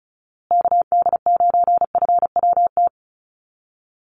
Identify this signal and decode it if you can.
Morse code, heard as a keyed tone, about 34 wpm, 700 Hz. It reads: KD9FWT
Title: USA Call Signs Repeated 2x 35wpm